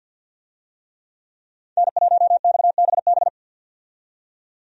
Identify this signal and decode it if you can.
Morse code, heard as a keyed tone, about 50 wpm, 700 Hz. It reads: N1XBB
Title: USA Call Signs - 5 Characters - 50wpm